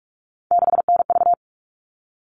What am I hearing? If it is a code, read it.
Morse code, heard as a keyed tone, about 44 wpm, 700 Hz. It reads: 6NV
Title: All Cut Numbers - 3-Digit 45wpm